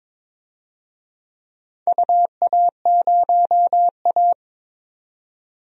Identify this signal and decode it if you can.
Morse code, heard as a keyed tone, about 22 wpm, 700 Hz. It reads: UA0A
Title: DXCC-100 Call Signs - 4 Characters - 22wpm